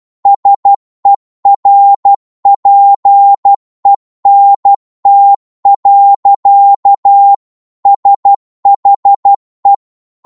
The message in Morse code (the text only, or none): SERPENT. SHE